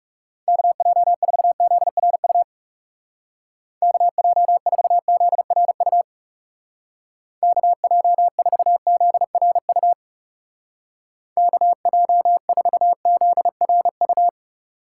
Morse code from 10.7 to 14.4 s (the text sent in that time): KJ4ZRU